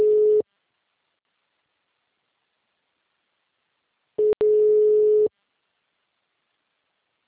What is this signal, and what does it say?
Ringback tone (Europe, 425 Hz).